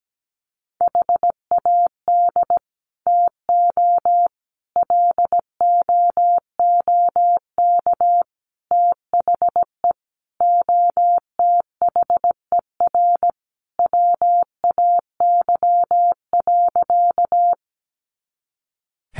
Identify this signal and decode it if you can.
Morse code, heard as a keyed tone, about 17 wpm, 700 Hz. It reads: HAD TO LOOK THE OTHER WAY.
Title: Sentences from Top 100 Words 17wpm